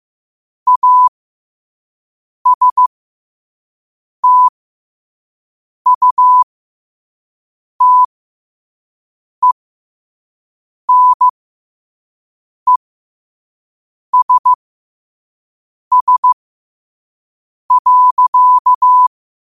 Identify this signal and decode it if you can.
Morse code, heard as a keyed tone, about 15 wpm, spaced as for 5 wpm, 1000 Hz. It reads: ASTUTENESS.
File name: qotdfiles / 5wpm